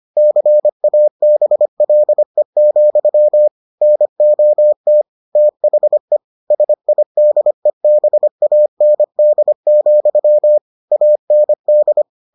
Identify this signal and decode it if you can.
Morse code, heard as a keyed tone, about 25 wpm, 600 Hz. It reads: CABLE, NOT THE SIDEBAND, AND